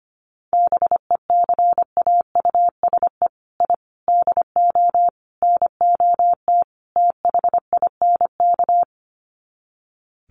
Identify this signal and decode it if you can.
Morse code, heard as a keyed tone, about 25 wpm, 700 Hz. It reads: BECAUSE I DO NOT THINK
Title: Sets of 5 Words Encore 25wpm